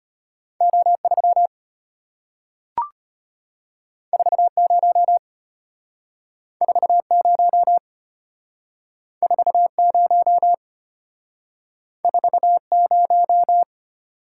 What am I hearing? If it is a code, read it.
Morse code, heard as a keyed tone, about 34 wpm, 700 Hz. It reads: O3 40 40 40 40